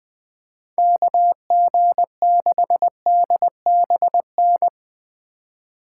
Morse code, heard as a keyed tone, about 20 wpm, 700 Hz. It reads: KG6DBN
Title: Speed Racing - USA Call Signs 17wpm